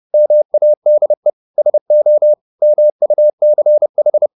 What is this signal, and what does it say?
Morse code, heard as a keyed tone, about 30 wpm, 600 Hz. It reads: MADE SO MUCH
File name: the old man and the sea / the old man and the sea 1 30wpm